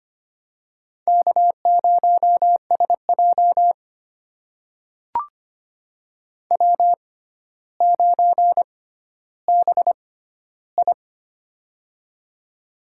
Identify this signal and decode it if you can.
Morse code, heard as a keyed tone, about 25 wpm, spaced as for 13 wpm, 700 Hz. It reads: K0SJ W9BI
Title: USA Call Signs - 4 Characters - 6x Character Spacing - 25wpm